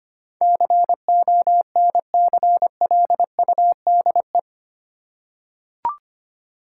Morse code, heard as a keyed tone, about 25 wpm, 700 Hz. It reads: CONCLUDE
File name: Eight Letter Words - 25wpm